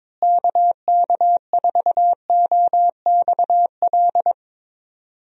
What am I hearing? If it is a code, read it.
Morse code, heard as a keyed tone, about 22 wpm, 700 Hz. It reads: KK4OXL
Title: USA Call Signs - 6 Characters - Repeated 2x - 22wpm